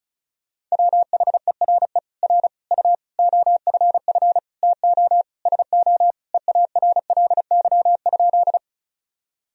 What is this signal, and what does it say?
Morse code, heard as a keyed tone, about 36 wpm, 700 Hz. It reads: WHERE R U OFF TO SO EARLY?
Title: Sentences from Top 400 Words and Abbreviations 35wpm